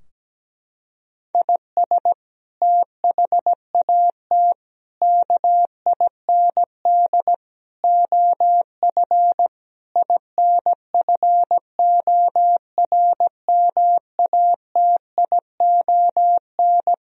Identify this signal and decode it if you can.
Morse code, heard as a keyed tone, about 17 wpm, 700 Hz. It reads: IS THAT KIND OF INFORMATION
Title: Sentences from Top 1000 Words - Review Part 3 - 17wpm